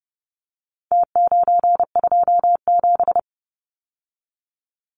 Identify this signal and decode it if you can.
Morse code, heard as a keyed tone, about 30 wpm, 700 Hz. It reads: T927
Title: Numbers - Contest Style - 4-Digit 30wpm